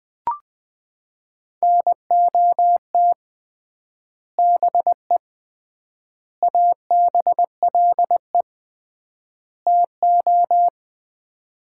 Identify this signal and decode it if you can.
Morse code, heard as a keyed tone, about 20 wpm, 700 Hz. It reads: NOT BE ABLE TO
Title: 3x Word-Spacing - Sets of 4 Words - 20wpm